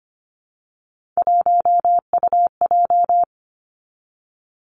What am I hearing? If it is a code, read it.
Morse code, heard as a keyed tone, about 25 wpm, 700 Hz. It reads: 1UJ
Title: Three Challenging Letter-Number Combinations 25wpm